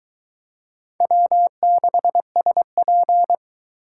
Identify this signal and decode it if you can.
Morse code, heard as a keyed tone, about 21 wpm, 700 Hz. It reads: W6SP